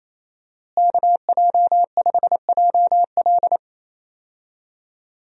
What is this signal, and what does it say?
Morse code, heard as a keyed tone, about 27 wpm, 700 Hz. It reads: KJ5JL
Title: USA Call Signs - 5 Characters - Repeated 3x - 28wpm